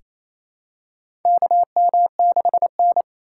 Morse code, heard as a keyed tone, about 28 wpm, 700 Hz. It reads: KM6N
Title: USA Call Signs - 4 Characters - 2x Character Spacing - 28wpm